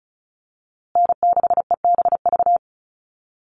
Morse code, heard as a keyed tone, about 34 wpm, 700 Hz. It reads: N6EBV
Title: USA Call Signs Repeated 3x 35wpm